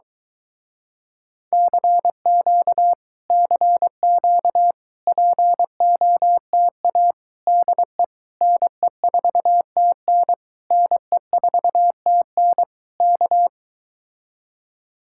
Morse code, heard as a keyed tone, about 23 wpm, 700 Hz. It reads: CQ CQ POTA DE NE4TN NE4TN K